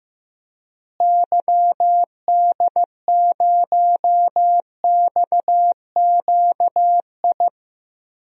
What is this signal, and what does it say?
Morse code, heard as a keyed tone, about 15 wpm, 700 Hz. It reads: YD0XQI
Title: DXCC-100 Call Signs - 6 Characters - Repeated 2x - 15wpm